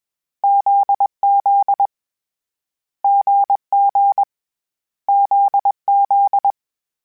Morse code, heard as a keyed tone, about 21 wpm, 800 Hz. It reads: ZZ GG ZZ
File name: Short Runs / Short Runs-mp3s / GZ GZ7 @20